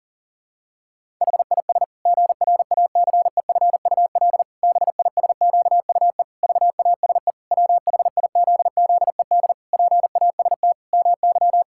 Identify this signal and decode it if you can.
Morse code, heard as a keyed tone, about 40 wpm, 700 Hz. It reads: HIS GRACEFUL BISQUE VASE WHIZZED PAST MY
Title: Pangram Challenge v2 - 40wpm